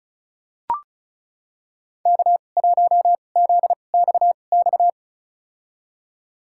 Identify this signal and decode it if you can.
Morse code, heard as a keyed tone, about 36 wpm, spaced as for 25 wpm, 700 Hz. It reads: K1ZXX
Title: USA Call Signs - 5 Characters - 2x Character Spacing - 35wpm